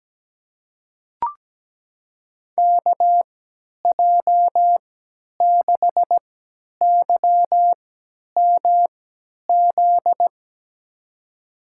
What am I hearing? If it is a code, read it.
Morse code, heard as a keyed tone, about 17 wpm, spaced as for 10 wpm, 700 Hz. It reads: KJ6YMZ
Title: USA Call Signs - 3x Character Spacing - 17wpm